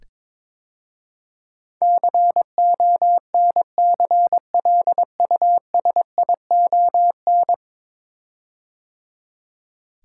Morse code, heard as a keyed tone, about 22 wpm, 700 Hz. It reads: CONCLUSION
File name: practice / 22wpm / 10 Letter Words - First 3 Letters Given - 22wpm